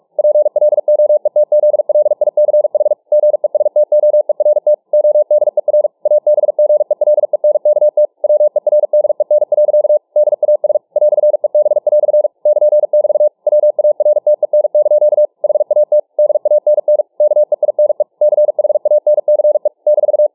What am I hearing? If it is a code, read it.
Morse code, heard as a keyed tone, about 45 wpm, 600 Hz. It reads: PLOETZLICH GESTOERT ODER ABGELENKT WERDEN. DAS "EB" <KN><BT> WARTEN) HAT DANN KEINE CHANCE -